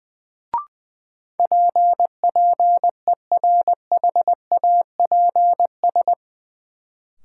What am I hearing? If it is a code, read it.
Morse code, heard as a keyed tone, about 20 wpm, 700 Hz. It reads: PPERHAPS
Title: Sentences from Top 1100 Words - New Word Drill Alt - 20wpm